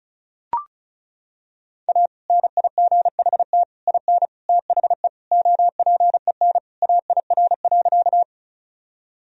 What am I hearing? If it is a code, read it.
Morse code, heard as a keyed tone, about 36 wpm, 700 Hz. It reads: A NIGHT IN THE OPEN AIR.
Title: Sentences from Top 1000 Words - Review Part 1 - 35wpm